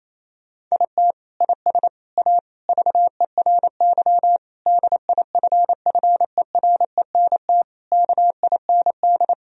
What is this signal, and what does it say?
Morse code, heard as a keyed tone, about 27 wpm, 700 Hz. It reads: IT IS A VERY DIFFERENT KIND
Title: Sentences from Top 1000 Words - Review Part 4 - 28wpm